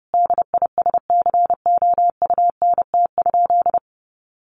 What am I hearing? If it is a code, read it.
Morse code, heard as a keyed tone, about 30 wpm, 700 Hz. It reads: DISCOUNT?
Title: Sentences from Top 1200 Words 30wpm